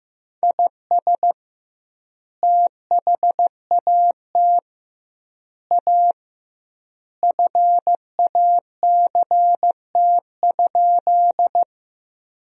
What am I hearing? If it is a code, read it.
Morse code, heard as a keyed tone, about 15 wpm, 700 Hz. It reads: IS THAT A FACT?
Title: 2x Word-Spacing - Sentences from Top 500 Words - 15wpm